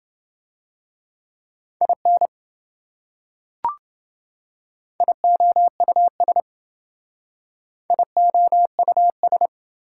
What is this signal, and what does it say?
Morse code, heard as a keyed tone, about 28 wpm, 700 Hz. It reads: IN IOUS IOUS